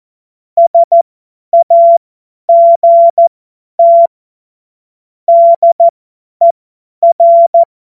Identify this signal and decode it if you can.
Morse code, heard as a keyed tone, about 14 wpm, spaced as for 10 wpm, 680 Hz. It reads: SAGT DER
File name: Morsekurs / Witze / Witz 31.Oktober 2023 50 F70